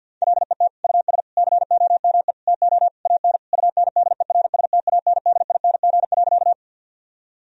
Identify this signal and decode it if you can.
Morse code, heard as a keyed tone, about 50 wpm, 700 Hz. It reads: LET US COME TO AN UNDERSTANDING.
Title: Sentences from Top 1000 Words - Review Part 3 - 50wpm